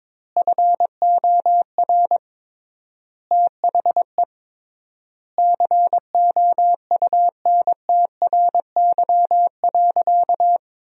Morse code, heard as a keyed tone, about 23 wpm, 700 Hz. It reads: FOR THE COUNTRY.
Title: 3x Word-Spacing - Sentences from Top 300 Words - 22wpm